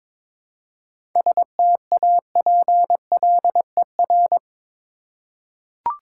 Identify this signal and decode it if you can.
Morse code, heard as a keyed tone, about 22 wpm, 700 Hz. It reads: STAPLER
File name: Difficult Words with Hints - 22wpm